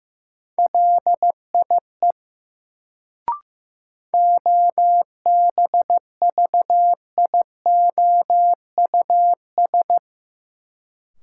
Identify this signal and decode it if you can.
Morse code, heard as a keyed tone, about 15 wpm, 700 Hz. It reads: LIE OBVIOUS
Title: Sentences from Top 1600 Words - New Word Drill - 15wpm